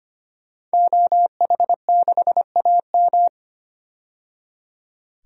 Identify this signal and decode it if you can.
Morse code, heard as a keyed tone, about 25 wpm, 700 Hz. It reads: OH6AM
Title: DXCC-100 Call Signs - 5 Characters - 25wpm